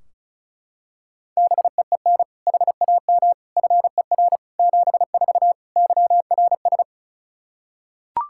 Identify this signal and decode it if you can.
Morse code, heard as a keyed tone, about 36 wpm, 700 Hz. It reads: BEEN HAM FER 74 YRS
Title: K1USN SST QSO Practice 35wpm